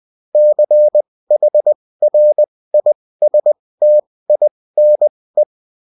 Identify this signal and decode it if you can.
Morse code, heard as a keyed tone, about 20 wpm, spaced as for 16 wpm, 600 Hz. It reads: CHRISTINE